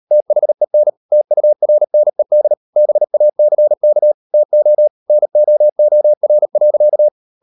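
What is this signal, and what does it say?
Morse code, heard as a keyed tone, about 38 wpm, 600 Hz. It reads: THEN TURNED BACK TO NOOR.
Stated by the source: Morse code practice at 38 WPM, sending Ken Follett's novel Never in English, from CW med Gnister.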